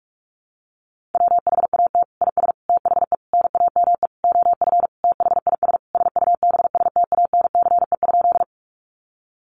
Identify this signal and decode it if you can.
Morse code, heard as a keyed tone, about 44 wpm, 700 Hz. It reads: WHAT IS THE NAME OF THIS SUBSTANCE?
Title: Sentences from Top 1500 Words 45wpm